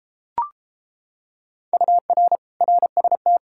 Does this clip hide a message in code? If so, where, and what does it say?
Yes: Morse code from 1.7 to 3.4 s, heard as a keyed tone, about 33 wpm, 700 Hz. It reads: UR RST